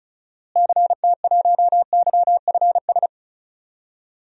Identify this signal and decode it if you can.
Morse code, heard as a keyed tone, about 34 wpm, 700 Hz. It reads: CT1YFS
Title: DXCC-100 Call Signs - 6 Characters - Repeated 3x - 35wpm